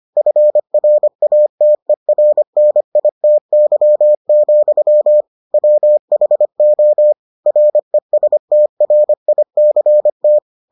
Morse code, heard as a keyed tone, about 25 wpm, 600 Hz. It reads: FRATERNITY, WHO RESTRICT